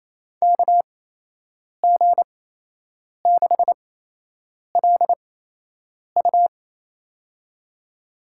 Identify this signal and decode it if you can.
Morse code, heard as a keyed tone, about 28 wpm, spaced as for 8 wpm, 700 Hz. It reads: KG6LU
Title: USA Call Signs - 5 Characters - 8x Character Spacing - 28wpm